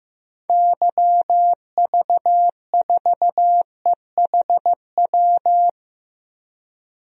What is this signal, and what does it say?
Morse code, heard as a keyed tone, about 15 wpm, 700 Hz. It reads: YV4EHW
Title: DXCC-100 Call Signs - 6 Characters - 15wpm